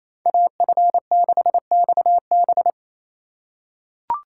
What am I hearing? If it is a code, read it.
Morse code, heard as a keyed tone, about 28 wpm, 700 Hz. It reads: AF6XB
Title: USA Call Signs - 5 Characters - 4x Character Spacing - 28wpm